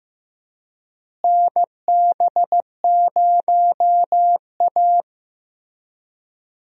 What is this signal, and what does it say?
Morse code, heard as a keyed tone, about 15 wpm, 700 Hz. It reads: NB0A
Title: USA Call Signs - 4 Characters - Repeated 3x - 15wpm